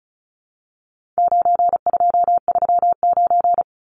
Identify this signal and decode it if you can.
Morse code, heard as a keyed tone, about 34 wpm, 700 Hz. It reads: 9239
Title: Numbers - 4-Digit 35wpm